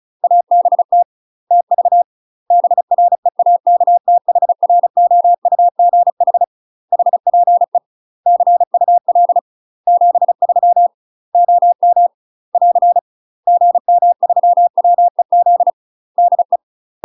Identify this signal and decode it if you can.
Morse code, heard as a keyed tone, about 34 wpm, 700 Hz. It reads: ABT TV BREAKTHROUGH HPE CUL 73 OM <AR> GM3WEZ DE
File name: UK MASTER QSOs TRAINING / 35WPM / UK MASTER QSOs 0004